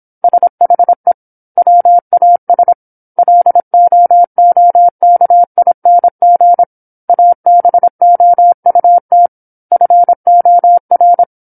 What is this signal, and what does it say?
Morse code, heard as a keyed tone, about 27 wpm, 700 Hz. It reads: SHE WAS LOOKING ABOUT FOR
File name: 08-Chapter-08 26 WPM